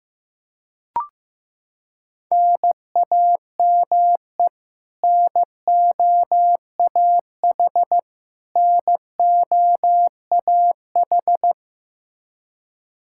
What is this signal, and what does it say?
Morse code, heard as a keyed tone, about 15 wpm, 700 Hz. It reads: NAME NOAH NOAH
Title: Practice QSO - Exchange 1 - Beginning - 15wpm